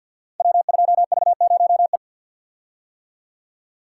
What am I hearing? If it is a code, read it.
Morse code, heard as a keyed tone, about 47 wpm, 700 Hz. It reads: W2V0E
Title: Code Groups - 5 Letter:Numbers - 50wpm